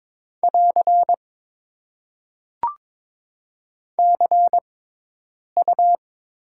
Morse code, heard as a keyed tone, about 23 wpm, spaced as for 8 wpm, 700 Hz. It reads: <AR> CU
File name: Top 90 QSO Elements - 6x Character Spacing - 22wpm